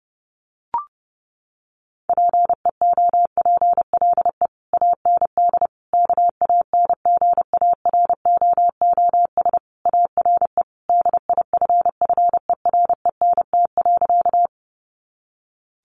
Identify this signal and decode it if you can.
Morse code, heard as a keyed tone, about 30 wpm, 700 Hz. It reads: PEOPLE AND KANGAROOS ARE DIFFERENT.
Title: Easy Phrases 30wpm